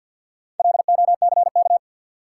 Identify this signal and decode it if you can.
Morse code, heard as a keyed tone, about 49 wpm, 700 Hz. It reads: LOCK
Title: Sentences from Top 1300 Words - New Word Drill Alt - 50wpm